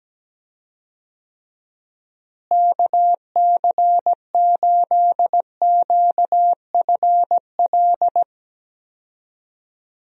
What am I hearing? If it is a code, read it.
Morse code, heard as a keyed tone, about 17 wpm, 700 Hz. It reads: KC8QFL
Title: USA Call Signs - 6 Characters - Repeated 3x - 17wpm